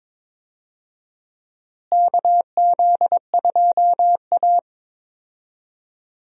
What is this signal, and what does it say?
Morse code, heard as a keyed tone, about 21 wpm, 700 Hz. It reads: KZ2A
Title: USA Call Signs - 4 Characters - Repeated 2x - 22wpm